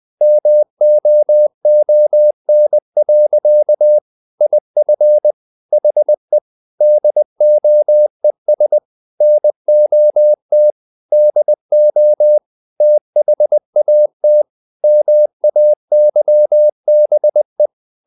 Morse code, heard as a keyed tone, about 20 wpm, 600 Hz. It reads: MOON. IF HE DOES NOT DO THAT MAYBE
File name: the old man and the sea / the old man and the sea part 1 20wpm